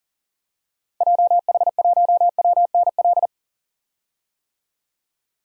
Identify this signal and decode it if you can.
Morse code, heard as a keyed tone, about 40 wpm, 700 Hz. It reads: JH1WNL
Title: DXCC-100 Call Signs - 6 Characters - Repeated 3x - 40wpm